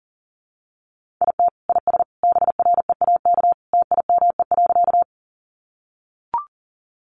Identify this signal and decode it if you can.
Morse code, heard as a keyed tone, about 40 wpm, 700 Hz. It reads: IT IS BREAK TIME.
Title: Easy Phrases 40wpm